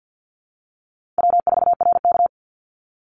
Morse code, heard as a keyed tone, about 48 wpm, 700 Hz. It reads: W4RK
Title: Easy Call Signs Repeated 2x - 50wpm